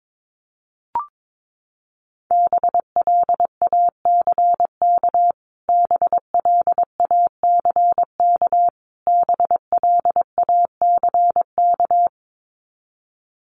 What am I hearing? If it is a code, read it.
Morse code, heard as a keyed tone, about 22 wpm, 700 Hz. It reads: BLACK BLACK BLACK
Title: Top 500 Words Repeated 22wpm